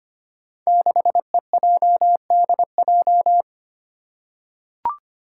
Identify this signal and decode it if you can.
Morse code, heard as a keyed tone, about 25 wpm, 700 Hz. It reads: 6EJDJ
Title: Five Challenging Letter-Number Combinations 25wpm